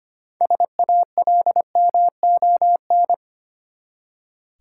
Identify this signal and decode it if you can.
Morse code, heard as a keyed tone, about 25 wpm, 690 Hz. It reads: SALMON
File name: practice / 25wpm / Easy Words with Hints - 25wpm